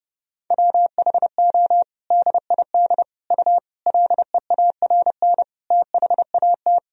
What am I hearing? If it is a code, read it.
Morse code, heard as a keyed tone, about 30 wpm, 700 Hz. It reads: WHO DID U LEARN THAT
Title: Sentences from Top 300 Words and Abbreviations 30wpm